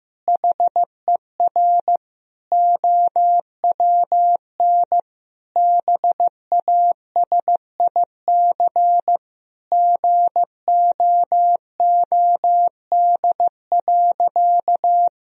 Morse code, heard as a keyed tone, about 16 wpm, 700 Hz. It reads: HER OWN BASIC GOOD.
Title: Sentences from Top 1000 Words - Review Part 3 - 15wpm